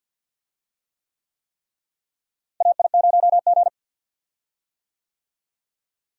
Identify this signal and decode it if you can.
Morse code, heard as a keyed tone, about 47 wpm, 700 Hz. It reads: AI0G
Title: USA Call Signs - 4 Characters - Repeated 3x - 50wpm